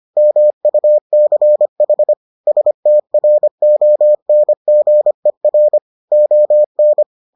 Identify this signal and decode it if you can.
Morse code, heard as a keyed tone, about 25 wpm, 600 Hz. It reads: MUCH STRONGER ON